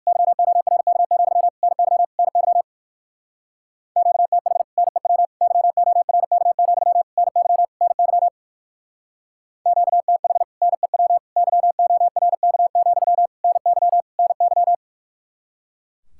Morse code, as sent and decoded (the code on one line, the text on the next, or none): -.-- --- .-. -.- --..-- / -. -.-- / -. -.-- / --.- - .... / -. .-- / -.-- --- .-. -.- --..-- / -. -.-- / -. -.-- / --.- - .... / -. .-- / -.-- --- .-. -.- --..-- / -. -.-- / -. -.--
YORK, NY NY QTH NEW YORK, NY NY QTH NEW YORK, NY NY